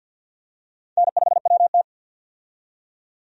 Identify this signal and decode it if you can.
Morse code, heard as a keyed tone, about 47 wpm, 700 Hz. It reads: N5WT
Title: USA Call Signs - 4 Characters - 50wpm